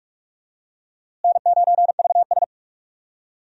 Morse code, heard as a keyed tone, about 43 wpm, 700 Hz. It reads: N9VS
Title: USA Call Signs - 4 Characters - Repeated 2x - 45wpm